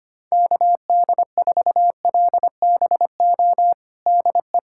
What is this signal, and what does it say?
Morse code, heard as a keyed tone, about 25 wpm, 700 Hz. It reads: KD4LBO DE
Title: USA Callsign DE Callsign - 6 Characters - 25wpm